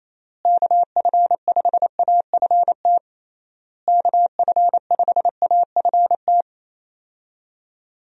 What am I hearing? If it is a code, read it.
Morse code, heard as a keyed tone, about 27 wpm, 700 Hz. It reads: KF5AFT KF5AFT
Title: USA Call Signs - 6 Characters - Repeated 3x - 28wpm